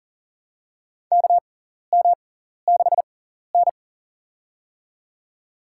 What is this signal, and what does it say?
Morse code, heard as a keyed tone, about 40 wpm, spaced as for 14 wpm, 700 Hz. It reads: KM6N
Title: USA Call Signs - 4 Characters - 6x Character Spacing - 40wpm